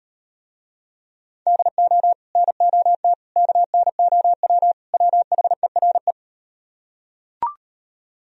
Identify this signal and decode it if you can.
Morse code, heard as a keyed tone, about 38 wpm, 700 Hz. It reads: DO NOT KNOW WHERE